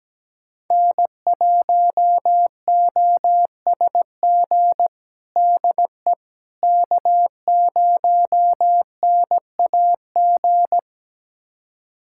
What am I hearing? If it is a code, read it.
Morse code, heard as a keyed tone, about 17 wpm, 700 Hz. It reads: N1OSG DE K0NAG
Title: USA Callsign DE Callsign - 5 Characters - 17wpm